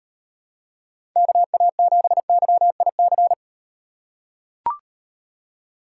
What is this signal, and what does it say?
Morse code, heard as a keyed tone, about 38 wpm, 700 Hz. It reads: KA7YIC